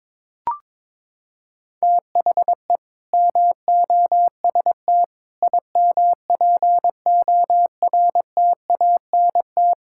Morse code, heard as a keyed tone, about 22 wpm, 700 Hz. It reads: THE MOST IMPORTANT
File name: practice / 22wpm / Sentences with 3-Word Phrases - 22wpm